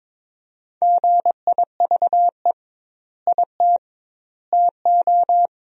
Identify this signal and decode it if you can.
Morse code, heard as a keyed tone, about 22 wpm, 700 Hz. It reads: GIVE IT TO
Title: 2x Word-Spacing - Sentences from Top 200 Words - 22wpm